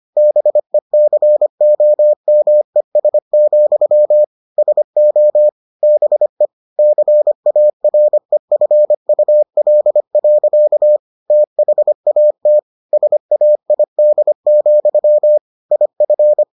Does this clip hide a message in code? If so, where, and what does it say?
Yes: Morse code from 0.2 to 16.4 s, heard as a keyed tone, about 25 wpm, 600 Hz. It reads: BECOMES, SO BE CAREFUL. THAT SAID, IF